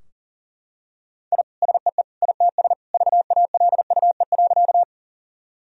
Morse code, heard as a keyed tone, about 40 wpm, 700 Hz. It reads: I SEE ITS VALUE.
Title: Sentences from Top 1000 Words - Review Part 3 - 40wpm